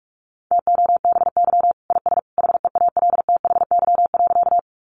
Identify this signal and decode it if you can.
Morse code, heard as a keyed tone, about 44 wpm, 700 Hz. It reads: TOBY IS HEALTHY.
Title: Easy Phrases 45wpm